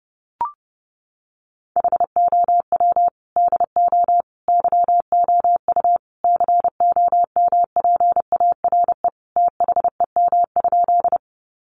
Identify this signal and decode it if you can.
Morse code, heard as a keyed tone, about 30 wpm, 700 Hz. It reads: HOW DO YOU COMPARE THEM?
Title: Sentences from Top 1300 Words 30wpm